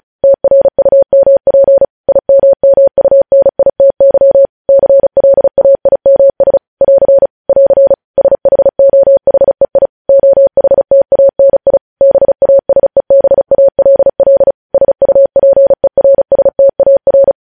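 Morse code, heard as a keyed tone, about 36 wpm, 550 Hz. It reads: TRUMP IMMUNITY CLAIMS <AR> <AR> SHOHEI OHTANI BASEBALL SUPERSTAR
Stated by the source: Morse code news headlines sent at 35 wpm.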